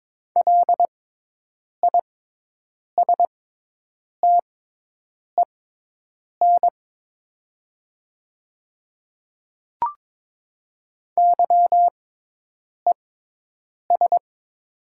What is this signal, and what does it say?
Morse code, heard as a keyed tone, about 22 wpm, spaced as for 8 wpm, 690 Hz. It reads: LISTEN YES
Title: 6x Character Spacing - Top 500 Words - 22wpm